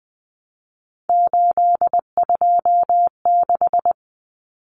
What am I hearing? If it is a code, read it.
Morse code, heard as a keyed tone, about 20 wpm, 700 Hz. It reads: 826
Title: Common Cut Numbers - 3-Digit 20wpm